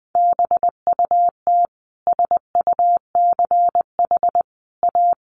Morse code, heard as a keyed tone, about 20 wpm, 700 Hz. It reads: BUT SUCH A
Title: Sentences from Top 1500 Words 20wpm